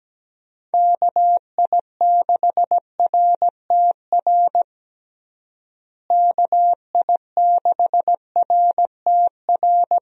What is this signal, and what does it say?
Morse code, heard as a keyed tone, about 17 wpm, 700 Hz. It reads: KI6RTR KI6RTR
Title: USA Call Signs - 6 Characters - Repeated 2x - 17wpm